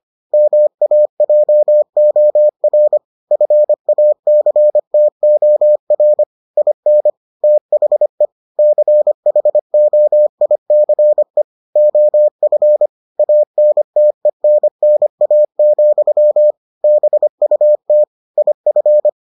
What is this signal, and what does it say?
Morse code, heard as a keyed tone, about 25 wpm, 600 Hz. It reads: MAJOR FACTOR IN THE CHOICE OF ANTENNA, BUT IF